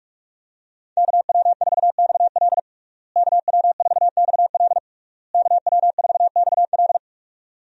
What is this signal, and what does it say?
Morse code, heard as a keyed tone, about 46 wpm, 700 Hz. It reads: KW4XL KW4XL KW4XL
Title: USA Call Signs - 5 Characters - Repeated 3x - 45wpm